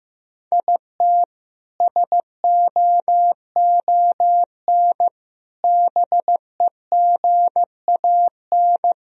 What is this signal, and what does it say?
Morse code, heard as a keyed tone, about 15 wpm, 700 Hz. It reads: IT SOON BEGAN
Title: Sentences from Top 1000 Words - Review Part 1 - 15wpm